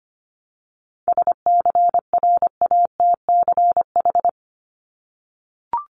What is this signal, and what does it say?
Morse code, heard as a keyed tone, about 25 wpm, 700 Hz. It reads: SCRATCH
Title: Top 1501-1750 Words 25wpm